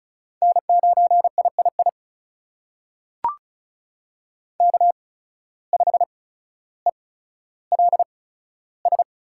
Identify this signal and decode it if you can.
Morse code, heard as a keyed tone, about 36 wpm, spaced as for 15 wpm, 700 Hz. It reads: N9III K5ELS
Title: USA Call Signs - 5 Characters - 8x Character Spacing - 35wpm